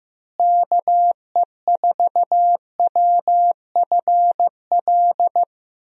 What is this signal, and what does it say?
Morse code, heard as a keyed tone, about 15 wpm, 700 Hz. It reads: KE4WFL
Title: USA Call Signs - 6 Characters - Repeated 3x - 15wpm